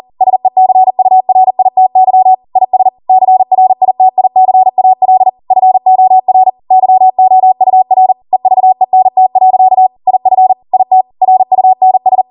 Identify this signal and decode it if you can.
Morse code, heard as a keyed tone, about 40 wpm, 750 Hz. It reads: SECURITY IS CRITICAL FOR YOUR EVENT. IF IT RUNS